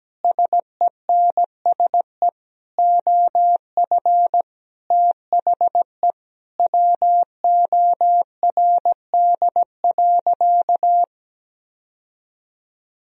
Morse code, heard as a keyed tone, about 17 wpm, 700 Hz. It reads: SENSE OF THE WORD.
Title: Sentences from Top 1000 Words - Review Part 4 - 17wpm